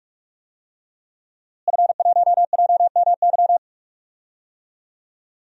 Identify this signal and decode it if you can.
Morse code, heard as a keyed tone, about 48 wpm, 700 Hz. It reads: F1JMY
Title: DXCC-100 Call Signs - 5 Characters - Repeated 2x - 45wpm